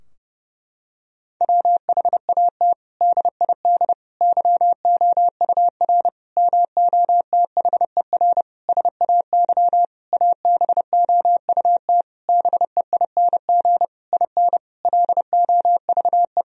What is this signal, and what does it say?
Morse code, heard as a keyed tone, about 30 wpm, 700 Hz. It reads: WHAT DID YOUR MOTHER SAY ABOUT BEING IN LOVE
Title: Sentences from Top 1000 Words - Review Part 3 - 30wpm